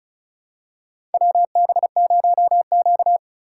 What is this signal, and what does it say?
Morse code, heard as a keyed tone, about 36 wpm, 700 Hz. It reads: WB0Q
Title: USA Call Signs - 4 Characters - 4x Character Spacing - 35wpm